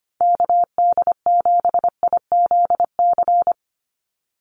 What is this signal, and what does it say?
Morse code, heard as a keyed tone, about 25 wpm, 690 Hz. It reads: KD7IZC
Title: USA Call Signs 25wpm